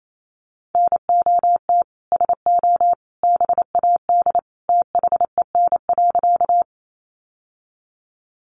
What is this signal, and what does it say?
Morse code, heard as a keyed tone, about 28 wpm, 700 Hz. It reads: NOT SO BAD THEN.
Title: Sentences from Top 700 Words 28wpm